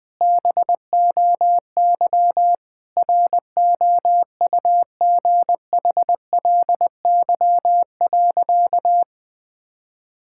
Morse code, heard as a keyed tone, about 19 wpm, 700 Hz. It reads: BOY ROUGHLY.
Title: Sentences from Top 1600 Words 20wpm